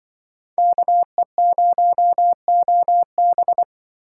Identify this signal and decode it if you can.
Morse code, heard as a keyed tone, about 24 wpm, 700 Hz. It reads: KE0OB
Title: Speed Racing - USA 2x2 Call Signs 20wpm